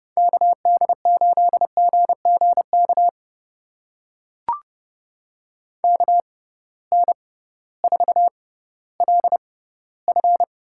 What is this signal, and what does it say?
Morse code, heard as a keyed tone, about 31 wpm, spaced as for 16 wpm, 700 Hz. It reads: KD8GGK KN4LF
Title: USA Call Signs - 6 Characters - 6x Character Spacing - 30wpm